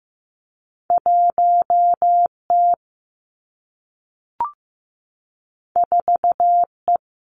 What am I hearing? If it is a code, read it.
Morse code, heard as a keyed tone, about 15 wpm, 700 Hz. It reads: 1 T 4 E
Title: Two Challenging Letter-Number Combinations 15wpm